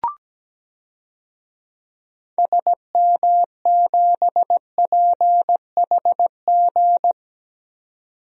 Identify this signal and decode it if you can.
Morse code, heard as a keyed tone, about 17 wpm, 700 Hz. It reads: SM7PHG
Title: DXCC-100 Call Signs - 6 Characters - 17wpm